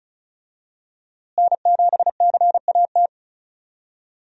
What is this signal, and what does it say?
Morse code, heard as a keyed tone, about 36 wpm, 700 Hz. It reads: N7CAT
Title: USA Call Signs - 5 Characters - Repeated 3x - 35wpm